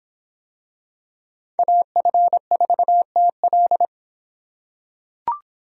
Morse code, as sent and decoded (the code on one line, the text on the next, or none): .- ..-. ....- - .-..
AF4TL